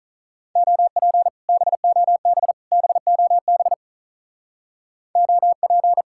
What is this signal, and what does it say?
Morse code, heard as a keyed tone, about 46 wpm, 700 Hz. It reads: OP BOB BOB OP